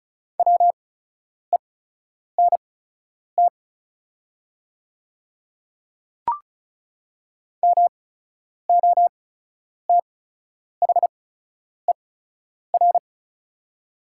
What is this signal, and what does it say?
Morse code, heard as a keyed tone, about 36 wpm, spaced as for 10 wpm, 700 Hz. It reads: WENT MOTHER
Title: 8x Character Spacing - Top 200 Words - 35wpm